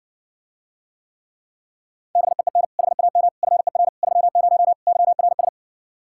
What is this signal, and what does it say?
Morse code, heard as a keyed tone, about 70 wpm, 700 Hz. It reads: BEEN HAM FER 30 YRS